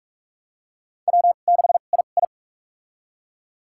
Morse code, heard as a keyed tone, about 46 wpm, spaced as for 32 wpm, 700 Hz. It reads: W6II
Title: USA Call Signs - 4 Characters - 2x Character Spacing - 45wpm